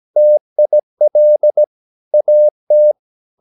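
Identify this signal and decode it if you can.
Morse code, heard as a keyed tone, about 17 wpm, 600 Hz. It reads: TIL AT
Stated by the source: Slow (QRS) Morse practice at 17 wpm, from CW med Gnister.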